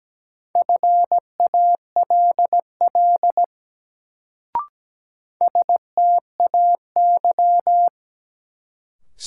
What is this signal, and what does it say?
Morse code, heard as a keyed tone, about 17 wpm, 700 Hz. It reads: FALL STAY
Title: Sentences from Top 500 Words - New Word Drill - 17wpm